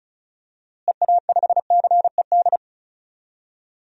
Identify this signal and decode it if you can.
Morse code, heard as a keyed tone, about 36 wpm, 700 Hz. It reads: EA5CED
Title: DXCC-100 Call Signs - 6 Characters - Repeated 3x - 35wpm